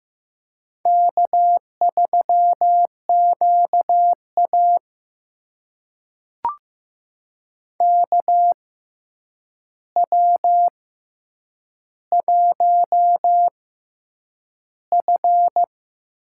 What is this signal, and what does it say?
Morse code, heard as a keyed tone, about 15 wpm, spaced as for 8 wpm, 700 Hz. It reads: K3QA KW1F
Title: USA Call Signs - 4 Characters - 6x Character Spacing - 15wpm